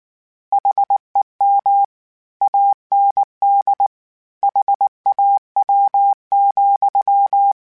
Morse code, heard as a keyed tone, about 19 wpm, 800 Hz. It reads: HEM AND HAW,
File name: Misc / Misc-mp3s / Word Pairs @18